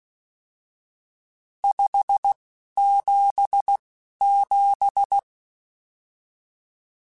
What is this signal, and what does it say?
Morse code, heard as a keyed tone, about 16 wpm, spaced as for 11 wpm, 800 Hz. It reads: 577
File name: RSTs / RST-mp3s / Various RST @10